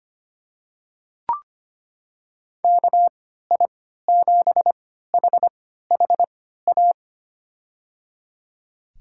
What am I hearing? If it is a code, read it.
Morse code, heard as a keyed tone, about 25 wpm, spaced as for 14 wpm, 700 Hz. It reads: KI7HHA
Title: USA Call Signs - 6 Characters - 3x Character Spacing - 25wpm